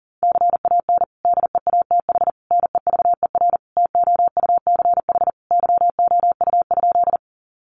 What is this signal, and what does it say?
Morse code, heard as a keyed tone, about 40 wpm, 700 Hz. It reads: CAN DEATH NEVER TOUCH YOU?
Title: Sentences from Top 1200 Words 40wpm